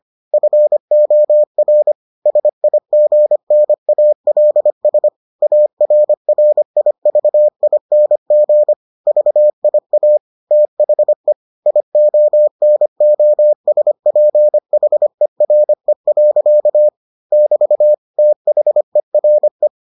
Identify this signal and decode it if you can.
Morse code, heard as a keyed tone, about 25 wpm, 600 Hz. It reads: FOR SIGNALS ARRIVING VIA THE IONOSPHERE. <BT> THERE